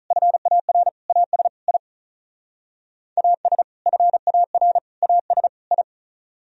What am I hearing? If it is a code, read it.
Morse code, heard as a keyed tone, about 43 wpm, 700 Hz. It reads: FAR AS I AS FAR AS I
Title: Speed-Racing - Sets of 4 Words 35wpm